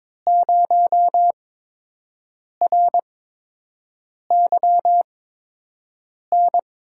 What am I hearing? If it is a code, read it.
Morse code, heard as a keyed tone, about 22 wpm, spaced as for 6 wpm, 700 Hz. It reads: 0RYN
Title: USA Call Signs - 8x Character Spacing - 22wpm